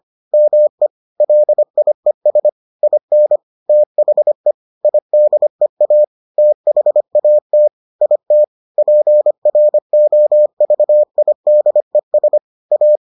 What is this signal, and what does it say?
Morse code, heard as a keyed tone, about 25 wpm, 600 Hz. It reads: ME LIES IN THE IDEA THAT IT PROVIDES A